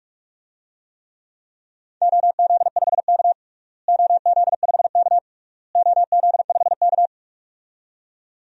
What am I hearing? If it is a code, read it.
Morse code, heard as a keyed tone, about 46 wpm, 700 Hz. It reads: OZ5K OZ5K OZ5K
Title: DXCC-100 Call Signs - 4 Characters - Repeated 3x - 45wpm